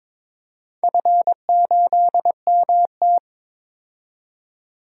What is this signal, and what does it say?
Morse code, heard as a keyed tone, about 22 wpm, 700 Hz. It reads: F8MT
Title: DXCC-100 Call Signs - 4 Characters - 22wpm